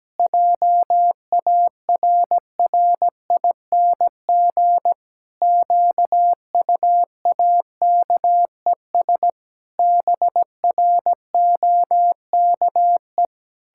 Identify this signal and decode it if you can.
Morse code, heard as a keyed tone, about 17 wpm, 700 Hz. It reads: JARRING QUAKES BROKE
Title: Pangram Challenge v2 - 17wpm